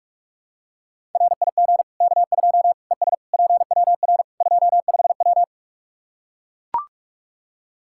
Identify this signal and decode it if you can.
Morse code, heard as a keyed tone, about 42 wpm, 700 Hz. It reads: RIG K2 ES PWR 25W
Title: Practice QSO - Exchange 2 - Info - 45wpm